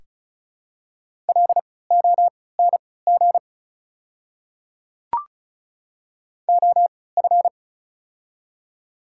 Morse code, heard as a keyed tone, about 36 wpm, spaced as for 20 wpm, 700 Hz. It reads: LONG OF
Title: 3x Character Spacing - Top 100 Words - 35wpm